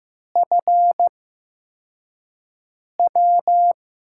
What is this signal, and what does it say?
Morse code, heard as a keyed tone, about 15 wpm, spaced as for 4 wpm, 700 Hz. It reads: FW
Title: USA Call Signs - 5 Characters - 8x Character Spacing - 15wpm